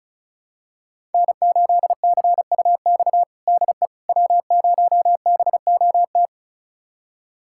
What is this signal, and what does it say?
Morse code, heard as a keyed tone, about 36 wpm, 700 Hz. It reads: N8CUX DE W0BOT
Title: USA Callsign DE Callsign - 5 Characters - 35wpm